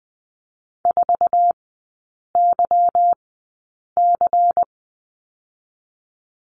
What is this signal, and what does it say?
Morse code, heard as a keyed tone, about 20 wpm, spaced as for 8 wpm, 700 Hz. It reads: 4YC
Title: 178 Lesson 29e - Slash - Three Characters 20wpm